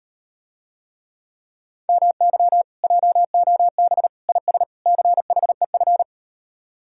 Morse code, heard as a keyed tone, about 38 wpm, 700 Hz. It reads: MY JOB IS CHEF